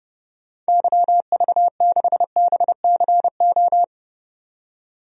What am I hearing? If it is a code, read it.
Morse code, heard as a keyed tone, about 29 wpm, 700 Hz. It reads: YV6BCO
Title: DXCC-100 Call Signs - 6 Characters - 30wpm